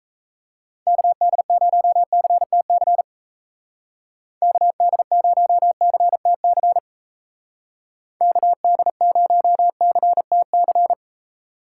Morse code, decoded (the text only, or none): KD0CTC KD0CTC KD0CTC